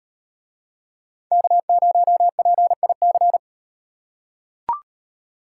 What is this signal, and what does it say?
Morse code, heard as a keyed tone, about 36 wpm, 700 Hz. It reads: K0PIC